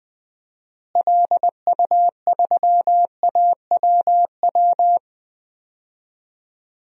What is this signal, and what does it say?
Morse code, heard as a keyed tone, about 20 wpm, 700 Hz. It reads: LU3AWW
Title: DXCC-100 Call Signs - 6 Characters - Repeated 2x - 20wpm